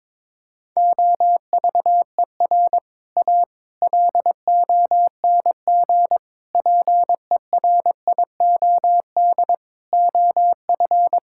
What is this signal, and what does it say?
Morse code, heard as a keyed tone, about 22 wpm, 700 Hz. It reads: OVER A LONG PERIOD OF
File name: practice / 22wpm / Sets of 6 Words - 22wpm